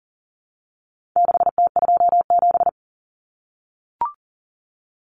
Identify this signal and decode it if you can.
Morse code, heard as a keyed tone, about 40 wpm, 700 Hz. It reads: 6T27
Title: Common Cut Numbers - 4-Digit 40wpm